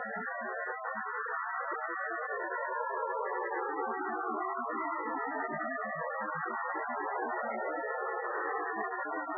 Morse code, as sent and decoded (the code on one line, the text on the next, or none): none